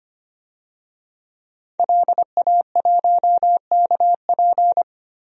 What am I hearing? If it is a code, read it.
Morse code, heard as a keyed tone, about 25 wpm, 700 Hz. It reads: LA1KP
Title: DXCC-100 Call Signs - 5 Characters - 25wpm